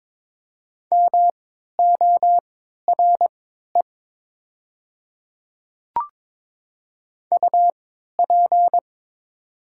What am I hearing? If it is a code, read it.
Morse code, heard as a keyed tone, about 22 wpm, spaced as for 12 wpm, 700 Hz. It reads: MORE UP
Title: 3x Character Spacing - Top 100 Words - 22wpm